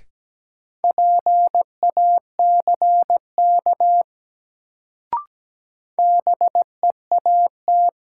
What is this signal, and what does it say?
Morse code, heard as a keyed tone, about 17 wpm, 700 Hz. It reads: PACK BEAT
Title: Sentences from Top 1300 Words - New Word Drill Alt - 17wpm